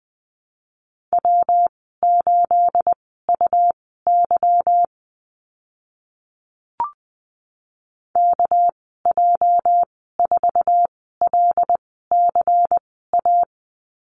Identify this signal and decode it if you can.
Morse code, heard as a keyed tone, about 20 wpm, spaced as for 14 wpm, 700 Hz. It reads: W8UY KJ4LCA
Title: 213 Lesson 34g - Number 8 - Call Signs 20wpm